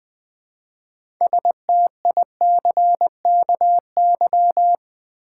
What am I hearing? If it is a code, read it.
Morse code, heard as a keyed tone, about 20 wpm, 700 Hz. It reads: STICKY
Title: 6 Letter Words - First 3 Letters Given - 20wpm